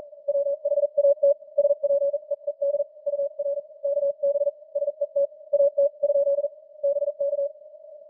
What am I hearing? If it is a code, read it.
Morse code, heard as a keyed tone, about 41 wpm, spaced as for 36 wpm, 570 Hz. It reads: WHAT SPEED UR KB SET AT? BK